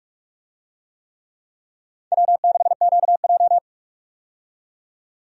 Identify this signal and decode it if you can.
Morse code, heard as a keyed tone, about 45 wpm, 700 Hz. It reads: W6QJ
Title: USA Call Signs - 4 Characters - 45wpm